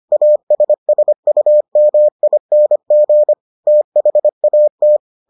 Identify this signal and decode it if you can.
Morse code, heard as a keyed tone, about 25 wpm, 600 Hz. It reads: ASSUMING THAT